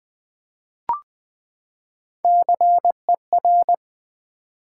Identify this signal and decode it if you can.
Morse code, heard as a keyed tone, about 21 wpm, 700 Hz. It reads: CER
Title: Build A Word - 3 Letters At A Time - 8-Letter Words - 20wpm